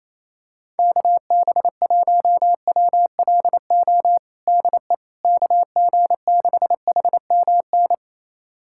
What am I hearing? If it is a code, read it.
Morse code, heard as a keyed tone, about 28 wpm, 700 Hz. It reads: KB1WLO DE KG6HMN
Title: USA Callsign DE Callsign 28wpm